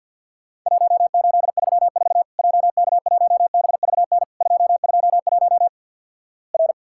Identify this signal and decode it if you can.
Morse code, heard as a keyed tone, about 50 wpm, 700 Hz. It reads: 1834 JK1BVN 221 R
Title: Worked All Europe DX Contest - QTC Exchange with Repeats - 50wpm